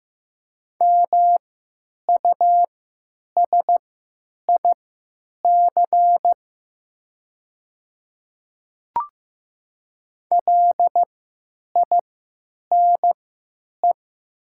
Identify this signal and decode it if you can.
Morse code, heard as a keyed tone, about 15 wpm, spaced as for 9 wpm, 700 Hz. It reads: MUSIC LINE
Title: 3x Character Spacing - Top 300 Words - 15wpm